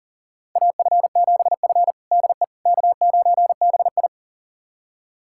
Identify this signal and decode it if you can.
Morse code, heard as a keyed tone, about 40 wpm, 700 Hz. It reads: AF7F DE K9BI
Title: USA Callsign DE Callsign - 4 Characters - 40wpm